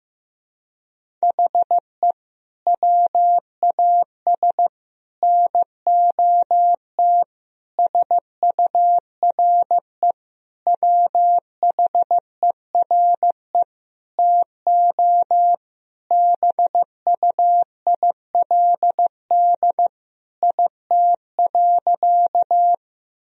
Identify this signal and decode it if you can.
Morse code, heard as a keyed tone, about 15 wpm, 700 Hz. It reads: HE WAS NOT SURE WHERE TO BUILD IT.
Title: Sentences from Top 1000 Words - Review Part 1 - 15wpm